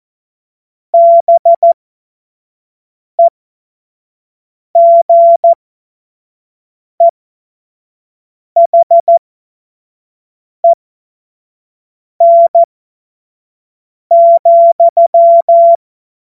Morse code, heard as a keyed tone, about 14 wpm, spaced as for 5 wpm, 680 Hz. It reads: BEGEHEN,
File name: Morsekurs / Witze / Witz 16 Januar 2024 25 F70